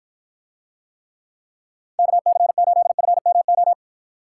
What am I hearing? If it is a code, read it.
Morse code, heard as a keyed tone, about 54 wpm, 700 Hz. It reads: KC8FMO